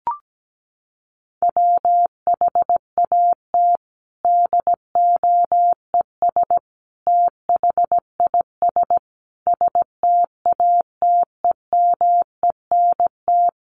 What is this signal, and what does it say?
Morse code, heard as a keyed tone, about 17 wpm, 700 Hz. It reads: WHAT DOES THIS STATEMENT
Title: Sentences from Top 1200 Words 17wpm